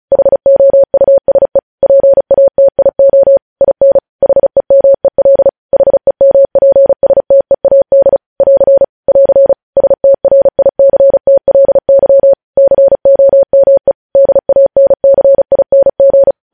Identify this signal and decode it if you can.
Morse code, heard as a keyed tone, about 36 wpm, 550 Hz. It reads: HOUSE PATIO IN HEMEL HEMPSTEAD <AR> <AR> STRICTLY COME DANCING